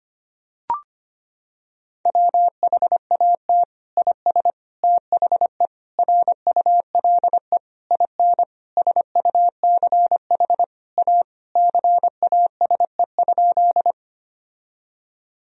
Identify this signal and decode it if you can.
Morse code, heard as a keyed tone, about 25 wpm, 700 Hz. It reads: WHAT IS THE RULE IN SUCH A CASE?
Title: Sentences from Top 1000 Words - Review Part 2 - 25wpm